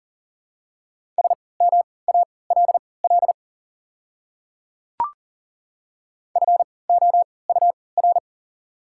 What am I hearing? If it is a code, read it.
Morse code, heard as a keyed tone, about 40 wpm, spaced as for 22 wpm, 690 Hz. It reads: SMALL FOUR
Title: 3x Character Spacing - Top 300 Words - 40wpm